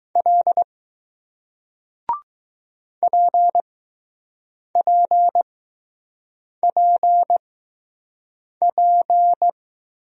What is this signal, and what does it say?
Morse code, heard as a keyed tone, about 20 wpm, spaced as for 6 wpm, 700 Hz. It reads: L PPPP